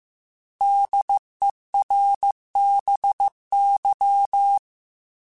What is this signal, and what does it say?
Morse code, heard as a keyed tone, about 15 wpm, 800 Hz. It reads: DERBY
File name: fromSMs / FromSMs mp3s / Cities@13